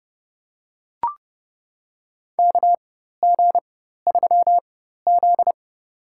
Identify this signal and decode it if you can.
Morse code, heard as a keyed tone, about 30 wpm, spaced as for 14 wpm, 700 Hz. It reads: KG3Z
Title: USA Call Signs - 4 Characters - 4x Character Spacing - 30wpm